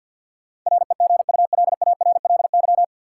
Morse code, heard as a keyed tone, about 53 wpm, 700 Hz. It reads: REGULARLY
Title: Sentences from Top 1400 Words - New Word Drill Alt - 50wpm